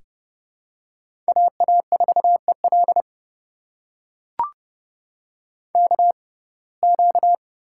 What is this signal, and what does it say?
Morse code, heard as a keyed tone, about 30 wpm, spaced as for 22 wpm, 700 Hz. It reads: AA4EL KQ
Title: USA Call Signs - 5 Characters - 6x Character Spacing - 30wpm